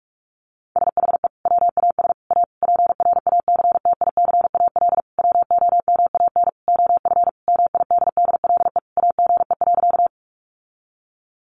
Morse code, heard as a keyed tone, about 44 wpm, 700 Hz. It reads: SHE WAS A PRACTICAL WOMAN OF MIDDLE AGE.
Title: Sentences from Top 1400 Words 45wpm